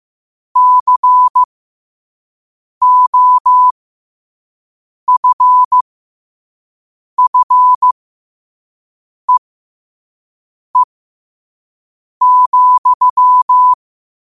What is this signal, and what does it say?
Morse code, heard as a keyed tone, about 15 wpm, spaced as for 5 wpm, 1000 Hz. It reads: COFFEE,
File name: qotdfiles / 5wpm